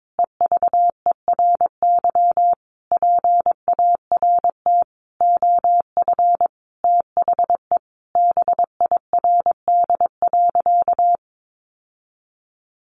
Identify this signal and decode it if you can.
Morse code, heard as a keyed tone, about 22 wpm, 700 Hz. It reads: EVERY PART OF THE BIRD.
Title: Sentences from Top 1500 Words 22wpm